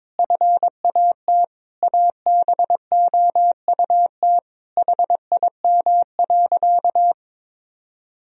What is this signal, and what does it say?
Morse code, heard as a keyed tone, about 22 wpm, 700 Hz. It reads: FAT ABOUT HIM.
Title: Sentences from Top 1000 Words - Review Part 4 - 22wpm